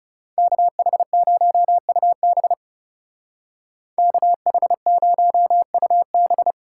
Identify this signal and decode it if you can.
Morse code, heard as a keyed tone, about 39 wpm, 700 Hz. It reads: KH0UB KH0UB